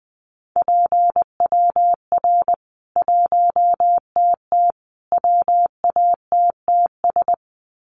Morse code, heard as a keyed tone, about 20 wpm, 700 Hz. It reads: PWR 1TT WATTS
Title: Common QSO Phrases 20wpm